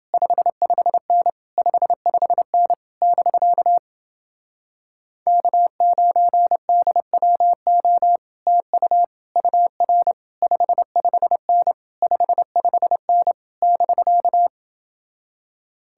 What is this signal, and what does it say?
Morse code, heard as a keyed tone, about 29 wpm, 700 Hz. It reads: 55N 55N <BK> K9DWO TU UR 55N 55N <BK>